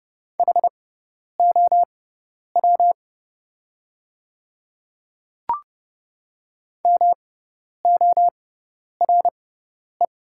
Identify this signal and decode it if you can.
Morse code, heard as a keyed tone, about 30 wpm, spaced as for 10 wpm, 700 Hz. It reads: HOW MORE
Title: 6x Character Spacing - Top 200 Words - 30wpm